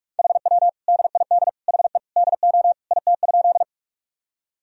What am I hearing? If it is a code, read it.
Morse code, heard as a keyed tone, about 46 wpm, 700 Hz. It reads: HW DID HE DO IT?
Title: Sentences from Top 100 Words and Abbreviations 45wpm